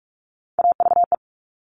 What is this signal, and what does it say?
Morse code, heard as a keyed tone, about 46 wpm, 700 Hz. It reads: AVE
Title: Top Three Letter Combinations in Words 45wpm